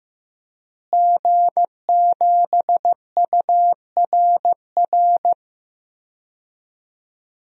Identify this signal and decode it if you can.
Morse code, heard as a keyed tone, about 16 wpm, 700 Hz. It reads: G7URR
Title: DXCC-100 Call Signs - 5 Characters - Repeated 3x - 15wpm